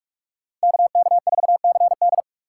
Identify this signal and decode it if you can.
Morse code, heard as a keyed tone, about 46 wpm, 700 Hz. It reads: KK4CD